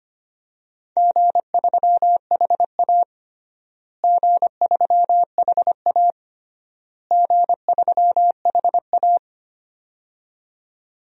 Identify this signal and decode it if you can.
Morse code, heard as a keyed tone, about 25 wpm, 700 Hz. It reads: G3HA G3HA G3HA
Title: DXCC-100 Call Signs - 4 Characters - Repeated 3x - 25wpm